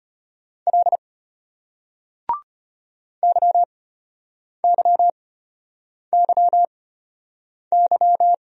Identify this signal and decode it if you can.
Morse code, heard as a keyed tone, about 33 wpm, spaced as for 8 wpm, 690 Hz. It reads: L YYYY